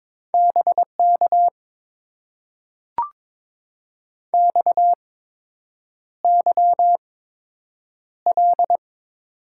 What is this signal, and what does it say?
Morse code, heard as a keyed tone, about 22 wpm, spaced as for 8 wpm, 690 Hz. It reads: BK XYL
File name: Top 90 QSO Elements - 8x Character Spacing - 22wpm